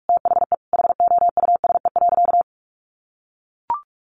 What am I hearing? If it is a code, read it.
Morse code, heard as a keyed tone, about 44 wpm, 700 Hz. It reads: THE HOUSE.
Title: Sentences from Top 1300 Words 45wpm